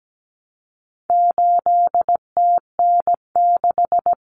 Morse code, heard as a keyed tone, about 17 wpm, 700 Hz. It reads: 8TN6
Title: Common Cut Numbers - 4-Digit 17wpm